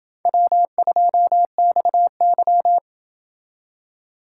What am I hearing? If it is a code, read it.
Morse code, heard as a keyed tone, about 28 wpm, 700 Hz. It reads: W2XY